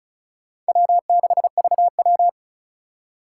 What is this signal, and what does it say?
Morse code, heard as a keyed tone, about 34 wpm, 690 Hz. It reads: W6VW
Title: USA Call Signs - 3x Character Spacing - 35wpm